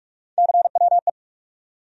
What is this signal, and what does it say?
Morse code, heard as a keyed tone, about 47 wpm, 700 Hz. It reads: CWE
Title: USA Call Sign Postfixes - 3 Letters 45wpm